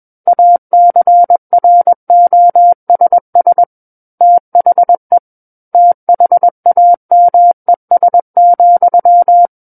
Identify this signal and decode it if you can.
Morse code, heard as a keyed tone, about 21 wpm, 700 Hz. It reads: ACROSS THE THAMES,
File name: Book One / 12-Chapter 12 - 21 WPM